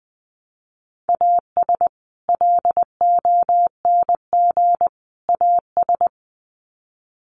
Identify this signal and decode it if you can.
Morse code, heard as a keyed tone, about 20 wpm, 700 Hz. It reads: AS LONG AS
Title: Sets of 3 Words 20wpm